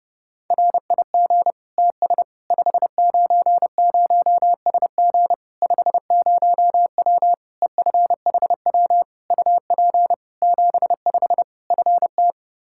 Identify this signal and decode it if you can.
Morse code, heard as a keyed tone, about 30 wpm, 700 Hz. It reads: RIG TS 590SG 50W EFHW UP 75 FT
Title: Practice QSO - Exchange 2 - Info - 30wpm